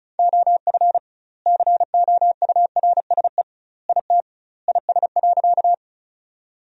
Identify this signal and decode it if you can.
Morse code, heard as a keyed tone, about 34 wpm, 700 Hz. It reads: OF COURSE IT IS.
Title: 2x Word-Spacing - Sentences from Top 500 Words - 35wpm